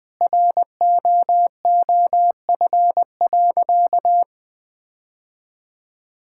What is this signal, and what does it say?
Morse code, heard as a keyed tone, about 20 wpm, 700 Hz. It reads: ROOF.
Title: Sentences with 3-Word Phrases - 20wpm